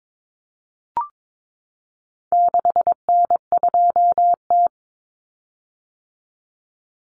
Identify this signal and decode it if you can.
Morse code, heard as a keyed tone, about 22 wpm, 700 Hz. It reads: 6N2T
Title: Common Cut Numbers - 4-Digit 22wpm